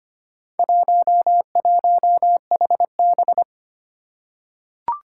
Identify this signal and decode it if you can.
Morse code, heard as a keyed tone, about 25 wpm, 700 Hz. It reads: 11HB
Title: Four Challenging Letter-Number Combinations 25wpm